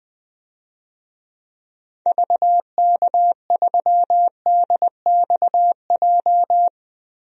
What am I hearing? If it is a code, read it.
Morse code, heard as a keyed tone, about 20 wpm, 700 Hz. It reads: VK3DXJ
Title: DXCC-100 Call Signs - 6 Characters - 20wpm